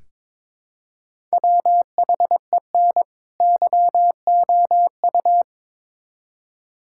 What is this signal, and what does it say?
Morse code, heard as a keyed tone, about 22 wpm, 700 Hz. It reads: WHEN YOU
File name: Build A Phrase - 2-Word Phrases - 22wpm